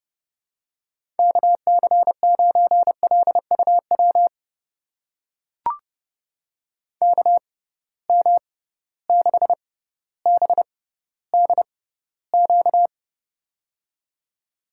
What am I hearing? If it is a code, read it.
Morse code, heard as a keyed tone, about 30 wpm, spaced as for 15 wpm, 700 Hz. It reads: KC9LUW KM6BDQ
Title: USA Call Signs - 6 Characters - 6x Character Spacing - 30wpm